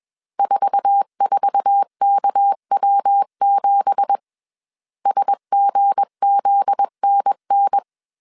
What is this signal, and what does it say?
Morse code, heard as a keyed tone, about 21 wpm, 800 Hz. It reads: 44KW7 SGZNN
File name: CW / DET / 1 9 1500C S C 20wpm 2310